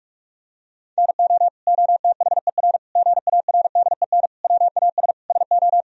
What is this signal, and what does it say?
Morse code, heard as a keyed tone, about 46 wpm, 700 Hz. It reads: NO OTHER GARDEN WAS SO
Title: Sentences from Top 1000 Words - Review Part 4 - 45wpm